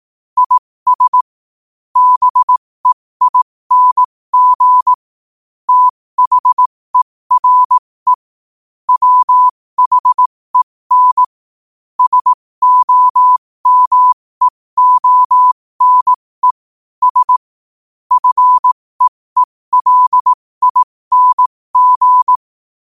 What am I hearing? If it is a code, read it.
Morse code, heard as a keyed tone, about 18 wpm, spaced as for 15 wpm, 1000 Hz. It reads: IS BEING THERE WHEN SOMEONES FEELING
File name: qotdfiles / 15wpm